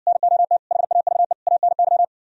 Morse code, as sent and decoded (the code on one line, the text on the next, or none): -. --- - / .... .- ...- . / .- -. -.--
NOT HAVE ANY